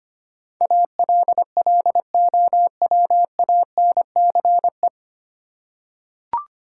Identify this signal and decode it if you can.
Morse code, heard as a keyed tone, about 25 wpm, 700 Hz. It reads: ALLOWANCE
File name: Nine Letter Words - 25wpm